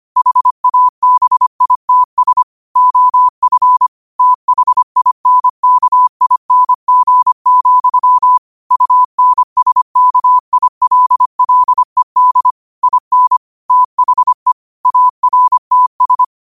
Morse code, heard as a keyed tone, about 25 wpm, 1000 Hz. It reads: HABITS OF THINKING, UNSKILLED IN THE ARTS